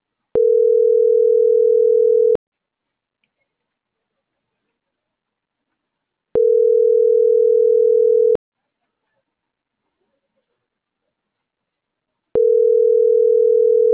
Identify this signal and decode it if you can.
Ringback tone (North America), 440 + 480 Hz.